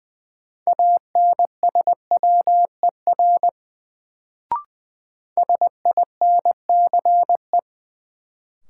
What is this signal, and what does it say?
Morse code, heard as a keyed tone, about 20 wpm, 700 Hz. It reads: ANSWER SINCE
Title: Sentences from Top 300 Words - New Word Drill - 20wpm